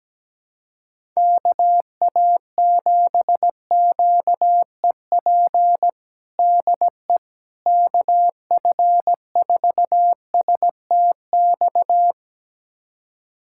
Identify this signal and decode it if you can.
Morse code, heard as a keyed tone, about 17 wpm, 700 Hz. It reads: KA7QEP DE KF4STX
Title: USA Callsign DE Callsign 17wpm